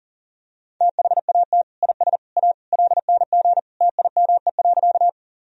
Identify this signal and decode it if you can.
Morse code, heard as a keyed tone, about 40 wpm, 700 Hz. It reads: THAT IS A LNG TIME.
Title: Sentences from Top 100 Words and Abbreviations 40wpm